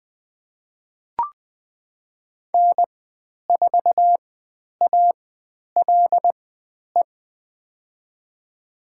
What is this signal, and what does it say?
Morse code, heard as a keyed tone, about 20 wpm, spaced as for 10 wpm, 700 Hz. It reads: N4ALE
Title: 109 Lesson 18g - Letter m - Call Signs 20wpm